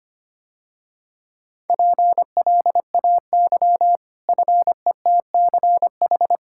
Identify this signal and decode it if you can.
Morse code, heard as a keyed tone, about 25 wpm, 700 Hz. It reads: PLAY FETCH
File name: Easy Sentences with Hints - 25wpm